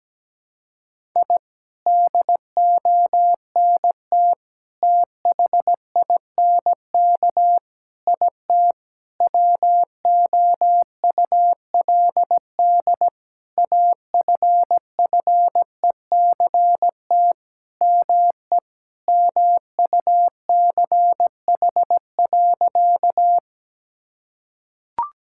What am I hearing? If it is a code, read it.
Morse code, heard as a keyed tone, about 17 wpm, 700 Hz. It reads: I DONT THINK IT WOULD AFFECT ME MUCH.
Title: Sentences from Top 1000 Words - Review Part 6 - 17wpm